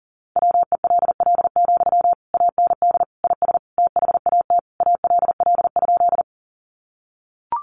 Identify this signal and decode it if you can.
Morse code, heard as a keyed tone, about 40 wpm, 700 Hz. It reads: WELL, AND IS THAT ALL?
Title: Sentences from Top 200 Words 40wpm